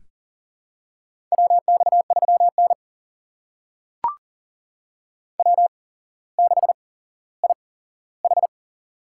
Morse code, heard as a keyed tone, about 40 wpm, spaced as for 17 wpm, 700 Hz. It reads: WX3N W6IH
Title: USA Call Signs - 4 Characters - 8x Character Spacing - 40wpm